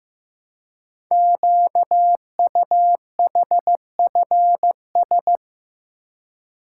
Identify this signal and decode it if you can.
Morse code, heard as a keyed tone, about 15 wpm, 700 Hz. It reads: QUHFS
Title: Code Groups - 5 Letters - 15wpm